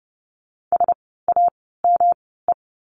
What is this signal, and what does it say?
Morse code, heard as a keyed tone, about 30 wpm, spaced as for 17 wpm, 700 Hz. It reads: SAME
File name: course / 30:10wpm / 108 Lesson 18f - Letter m - Words 30wpm